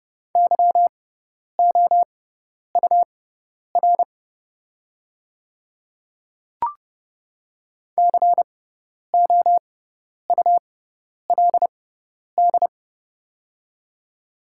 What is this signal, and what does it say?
Morse code, heard as a keyed tone, about 30 wpm, spaced as for 10 wpm, 700 Hz. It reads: YOUR COULD
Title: 6x Character Spacing - Top 200 Words - 30wpm